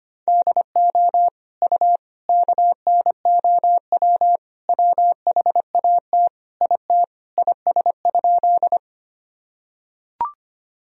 Morse code, heard as a keyed tone, about 25 wpm, 700 Hz. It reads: DO U KNOW WHAT IT IS?
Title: Sentences from Top 100 Words and Abbreviations 25wpm